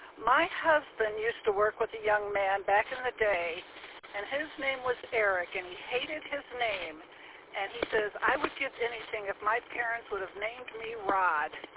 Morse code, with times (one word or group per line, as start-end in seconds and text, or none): none